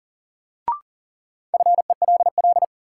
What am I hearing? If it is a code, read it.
Morse code, heard as a keyed tone, about 37 wpm, 700 Hz. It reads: FELL
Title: Sentences from Top 1600 Words - New Word Drill Alt - 40wpm